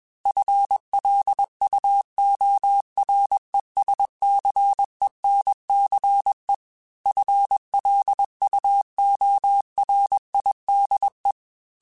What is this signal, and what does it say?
Morse code, heard as a keyed tone, about 21 wpm, 800 Hz. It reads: FLUORESCENCE FLUORIDE
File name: FL Runs / FL Runs mp3s / FL Runs @20